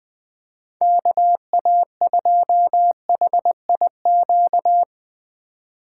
Morse code, heard as a keyed tone, about 20 wpm, 700 Hz. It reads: KA2HIQ
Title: USA Call Signs - 4x Character Spacing - 20wpm